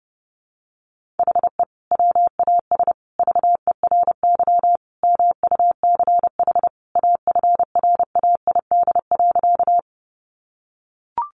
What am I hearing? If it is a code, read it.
Morse code, heard as a keyed tone, about 30 wpm, 700 Hz. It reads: HE WAS VERY MUCH AFRAID.
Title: Sentences from Top 1500 Words 30wpm